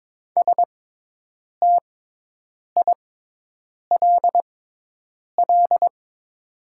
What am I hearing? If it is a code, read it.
Morse code, heard as a keyed tone, about 22 wpm, spaced as for 8 wpm, 700 Hz. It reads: STILL
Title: 6x Character Spacing - Top 300 Words - 22wpm